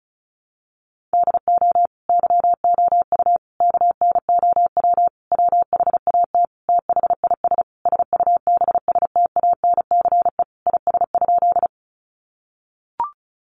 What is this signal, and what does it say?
Morse code, heard as a keyed tone, about 34 wpm, 700 Hz. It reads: DO YOU KNOW WHAT THIS SUBSTANCE IS?
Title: Sentences from Top 1500 Words 35wpm